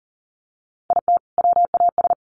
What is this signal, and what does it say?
Morse code, heard as a keyed tone, about 40 wpm, 700 Hz. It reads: IT WAS
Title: Sentences from Top 1400 Words 40wpm